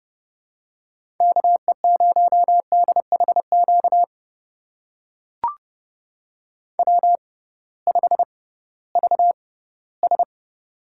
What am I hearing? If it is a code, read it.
Morse code, heard as a keyed tone, about 30 wpm, spaced as for 18 wpm, 700 Hz. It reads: KE0DHQ W5VS
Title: USA Call Signs - 6x Character Spacing - 30wpm